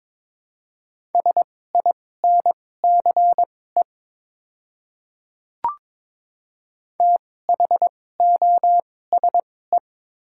Morse code, heard as a keyed tone, about 22 wpm, spaced as for 16 wpm, 700 Hz. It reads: SINCE THOSE
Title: 2x Character Spacing - Top 300 Words - 22wpm